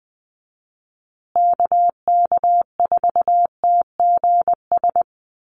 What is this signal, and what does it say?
Morse code, heard as a keyed tone, about 20 wpm, 700 Hz. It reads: KK4TGS
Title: USA Call Signs Repeated 3x 20wpm